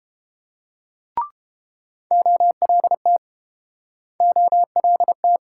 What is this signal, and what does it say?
Morse code, heard as a keyed tone, about 30 wpm, 700 Hz. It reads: OLT OLT